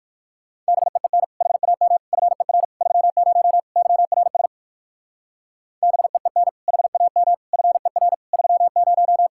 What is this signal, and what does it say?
Morse code, heard as a keyed tone, about 67 wpm, spaced as for 58 wpm, 700 Hz. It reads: BEEN HAM FER 30 YRS BEEN HAM FER 30